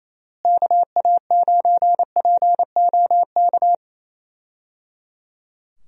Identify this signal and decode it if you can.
Morse code, heard as a keyed tone, about 28 wpm, 700 Hz. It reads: KA9POK
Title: USA Call Signs - 6 Characters - Repeated 3x - 28wpm